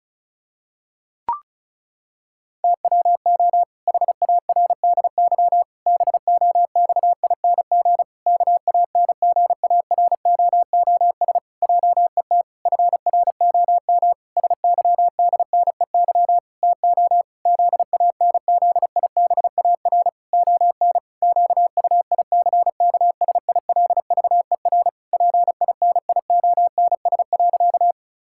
Morse code, heard as a keyed tone, about 36 wpm, 700 Hz. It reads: TWO HARDY BOXING KANGAROOS JET FROM SYDNEY TO ZANZIBAR ON QUICKSILVER PINIONS.
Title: Pangram Challenge v2 - 35wpm